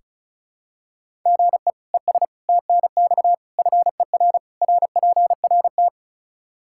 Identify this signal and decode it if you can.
Morse code, heard as a keyed tone, about 36 wpm, 700 Hz. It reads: GE ES TNX FER RPRT
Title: Practice QSO - Exchange 1 - Beginning - 35wpm